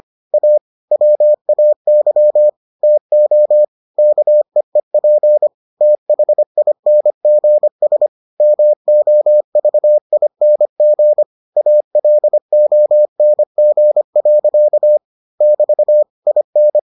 Morse code, heard as a keyed tone, about 25 wpm, 600 Hz. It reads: A WAY TO KEEP THINGS MOVING ALONG. <BT> IN